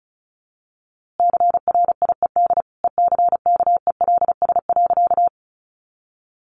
Morse code, heard as a keyed tone, about 34 wpm, 700 Hz. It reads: CRIED ECKELS.
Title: A Sound Of Thunder 35wpm